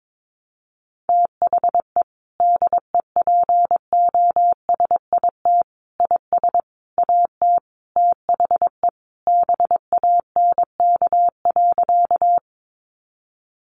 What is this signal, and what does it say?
Morse code, heard as a keyed tone, about 22 wpm, 700 Hz. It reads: THE DEPOSIT IS AT THE BANK.
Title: Sentences from Top 1300 Words 22wpm